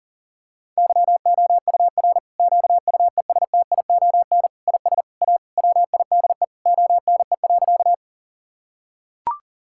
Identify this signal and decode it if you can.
Morse code, heard as a keyed tone, about 40 wpm, 700 Hz. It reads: YOUR QUESTION IS A WIDE ONE.
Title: Sentences from Top 1000 Words - Review Part 6 - 40wpm